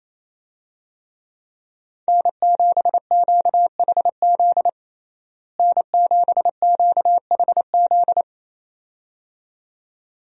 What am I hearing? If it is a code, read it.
Morse code, heard as a keyed tone, about 28 wpm, 700 Hz. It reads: N7QHZ N7QHZ
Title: USA Call Signs - 5 Characters - Repeated 2x - 28wpm